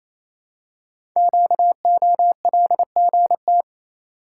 Morse code, heard as a keyed tone, about 29 wpm, 700 Hz. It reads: QOLGT
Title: Code Groups - 5 Letters - 28wpm